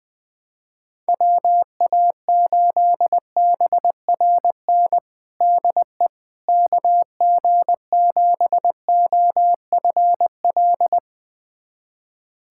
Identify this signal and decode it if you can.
Morse code, heard as a keyed tone, about 20 wpm, 700 Hz. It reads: WA8BRN DE KG7OFL
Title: USA Callsign DE Callsign - 6 Characters - 20wpm